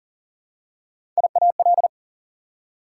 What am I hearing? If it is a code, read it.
Morse code, heard as a keyed tone, about 40 wpm, 700 Hz. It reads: IAL
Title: Prefixes and Suffixes 40wpm